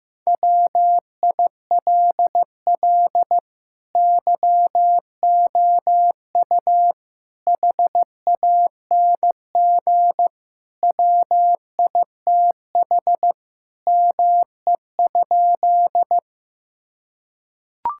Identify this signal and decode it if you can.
Morse code, heard as a keyed tone, about 15 wpm, 700 Hz. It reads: WILL YOU HANG WITH ME?
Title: Sentences from Top 1600 Words 15wpm